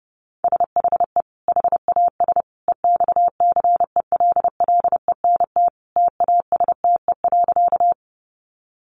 Morse code, heard as a keyed tone, about 30 wpm, 700 Hz. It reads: SHE HAS EXCELLENT TASTE.
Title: Sentences from Top 1100 Words 30wpm